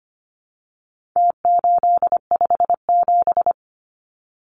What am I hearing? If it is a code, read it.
Morse code, heard as a keyed tone, about 25 wpm, 700 Hz. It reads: T857
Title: Numbers - Contest Style - 4-Digit 25wpm